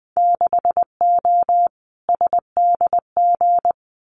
Morse code, heard as a keyed tone, about 20 wpm, 700 Hz. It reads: 6O SDG
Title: Winter Field Day Exchanges Encore 20wpm